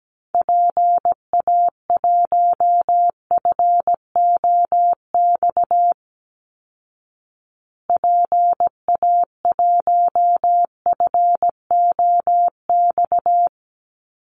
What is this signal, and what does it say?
Morse code, heard as a keyed tone, about 17 wpm, 700 Hz. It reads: PA1FOX PA1FOX
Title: Call Signs Repeated 2x - 17wpm